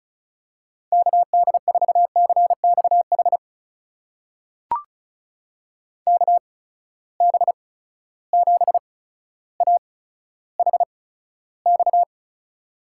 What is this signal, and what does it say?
Morse code, heard as a keyed tone, about 34 wpm, spaced as for 15 wpm, 700 Hz. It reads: KD4CXH KB7AHX
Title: USA Call Signs - 6 Characters - 8x Character Spacing - 35wpm